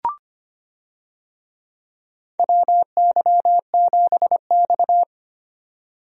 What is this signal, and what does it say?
Morse code, heard as a keyed tone, about 25 wpm, 700 Hz. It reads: WY7X
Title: USA Call Signs - 4 Characters - 25wpm